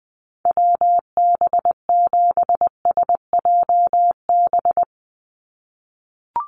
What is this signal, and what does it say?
Morse code, heard as a keyed tone, about 19 wpm, 700 Hz. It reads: WB7SJB
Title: USA Call Signs 20wpm